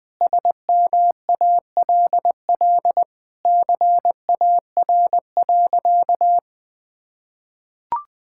Morse code, heard as a keyed tone, about 20 wpm, 700 Hz. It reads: SMALL CAR.
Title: Sentences with 4-Word Phrases - Spoken Beginning - 20wpm